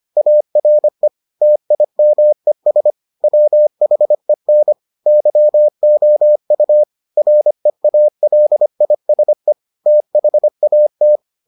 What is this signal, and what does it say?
Morse code, heard as a keyed tone, about 25 wpm, 600 Hz. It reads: ARE TIMES WHEN YOU REALISE THAT